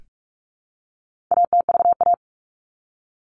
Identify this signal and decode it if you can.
Morse code, heard as a keyed tone, about 44 wpm, 700 Hz. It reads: ATVA
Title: All Cut Numbers - 4-Digit 45wpm